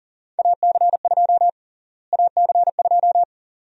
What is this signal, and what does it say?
Morse code, heard as a keyed tone, about 40 wpm, 700 Hz. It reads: AC2 AC2
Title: USA Call Sign Prefixes - Repeated 2x - 40wpm